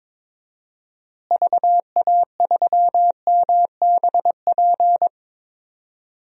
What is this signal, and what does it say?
Morse code, heard as a keyed tone, about 22 wpm, 700 Hz. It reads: VA3MBP
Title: DXCC-100 Call Signs - 6 Characters - Repeated 2x - 22wpm